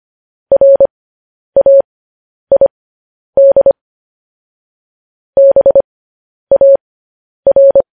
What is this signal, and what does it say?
Morse code, heard as a keyed tone, about 25 wpm, spaced as for 10 wpm, 550 Hz. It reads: RAID BAR
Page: news headlines in morse code at 10 wpm